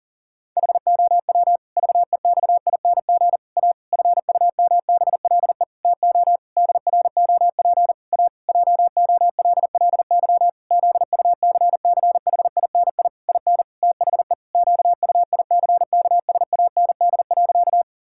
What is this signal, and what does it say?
Morse code, heard as a keyed tone, about 40 wpm, 700 Hz. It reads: HOW VEXING A FUMBLE TO DROP A JOLLY ZUCCHINI IN THE QUICKSAND.
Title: Pangram Challenge v2 - 40wpm